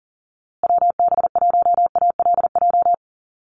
Morse code, heard as a keyed tone, about 40 wpm, 700 Hz. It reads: WB1ALJ
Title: USA Call Signs Repeated 2x 40wpm